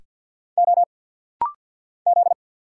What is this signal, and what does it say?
Morse code, heard as a keyed tone, about 50 wpm, 700 Hz. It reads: O Z
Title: Single Letter-Number Alt - Rapid-Fire 50wpm